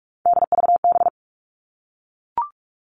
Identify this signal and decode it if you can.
Morse code, heard as a keyed tone, about 44 wpm, 700 Hz. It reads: DVB
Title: All Cut Numbers - 3-Digit 45wpm